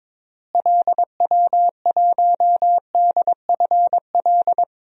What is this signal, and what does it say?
Morse code, heard as a keyed tone, about 22 wpm, 700 Hz. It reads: LW1DFL
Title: DXCC-100 Call Signs - 6 Characters - Repeated 3x - 22wpm